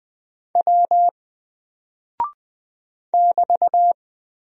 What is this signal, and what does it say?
Morse code, heard as a keyed tone, about 20 wpm, 700 Hz. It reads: W <BT>
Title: Single Character (Letter:Number:Punctuation:Prosign) Alt 20wpm